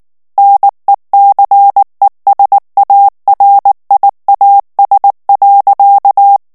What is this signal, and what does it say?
Morse code, heard as a keyed tone, about 19 wpm, 800 Hz. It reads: NECESARIAS.
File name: morse mp3 / ESPAGNOL TEXTE EN CLAIR 2 VITESSE 18 MOTS MINUTES